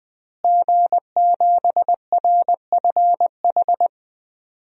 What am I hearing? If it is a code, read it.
Morse code, heard as a keyed tone, about 21 wpm, 700 Hz. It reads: G7RFH
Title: DXCC-100 Call Signs - 5 Characters - Repeated 2x - 20wpm